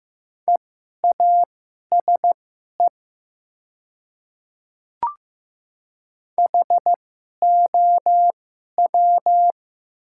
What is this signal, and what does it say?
Morse code, heard as a keyed tone, about 15 wpm, spaced as for 11 wpm, 700 Hz. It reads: EASE HOW
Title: 2x Character Spacing - Top 300 Words - 15wpm